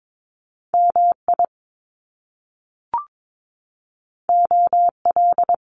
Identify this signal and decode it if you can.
Morse code, heard as a keyed tone, about 22 wpm, 700 Hz. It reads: MI OL
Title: Top Two Letter Combinations in Words 22wpm